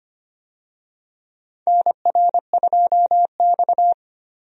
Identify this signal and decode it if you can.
Morse code, heard as a keyed tone, about 25 wpm, 700 Hz. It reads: NR2X
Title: USA Call Signs - 4 Characters - 25wpm